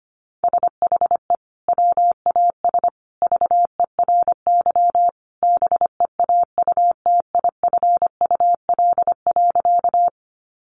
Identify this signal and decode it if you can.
Morse code, heard as a keyed tone, about 25 wpm, 700 Hz. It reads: SHE WAS VERY BEAUTIFUL.
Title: Sentences from Top 900 Words 25wpm